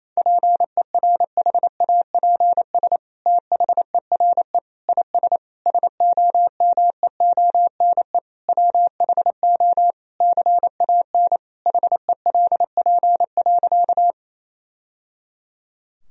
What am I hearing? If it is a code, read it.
Morse code, heard as a keyed tone, about 28 wpm, 700 Hz. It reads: PERHAPS THERE IS SOMEONE WHO CAN HELP.
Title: Sentences from Top 1000 Words - Review Part 4 - 28wpm